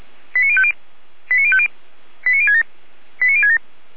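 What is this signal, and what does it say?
Five-tone selective call (ZVEI), heard as quick runs of five tones, tones of about 70 ms. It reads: 89050 89050 89067 89067
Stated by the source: Five-tone ZVEI selective call received by SDR.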